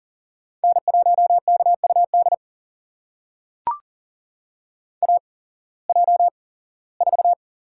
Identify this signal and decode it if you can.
Morse code, heard as a keyed tone, about 40 wpm, spaced as for 21 wpm, 700 Hz. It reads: N1KUD AJ4
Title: USA Call Signs - 5 Characters - 8x Character Spacing - 40wpm